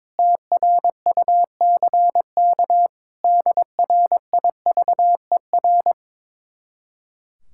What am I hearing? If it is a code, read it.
Morse code, heard as a keyed tone, about 22 wpm, 700 Hz. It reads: TRUCK DRIVER
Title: Speed-Racing Common QSO Phrases Encore - 22wpm